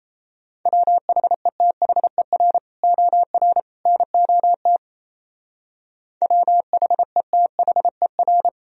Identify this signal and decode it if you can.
Morse code, heard as a keyed tone, about 35 wpm, 700 Hz. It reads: WHETHER OR NOT WHETHER
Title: Speed-Racing - Sets of 3 Words 28wpm